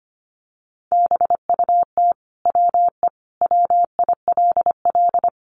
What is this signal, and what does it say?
Morse code, heard as a keyed tone, about 25 wpm, 700 Hz. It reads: BUT WE WILL
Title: Sentences from Top 1100 Words 25wpm